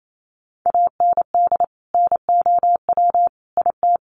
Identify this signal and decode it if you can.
Morse code, heard as a keyed tone, about 28 wpm, 700 Hz. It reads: AND NOW IT
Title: Sentences from Top 1100 Words 28wpm